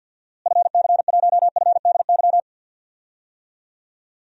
Morse code, heard as a keyed tone, about 52 wpm, 700 Hz. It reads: FC1FDY
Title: DXCC-100 Call Signs - 6 Characters - Repeated 3x - 50wpm